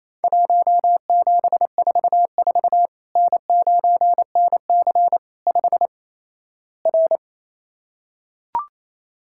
Morse code, heard as a keyed tone, about 28 wpm, 700 Hz. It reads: 1744 N9NC 5 R
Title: Worked All Europe DX Contest - QTC Exchange - 28wpm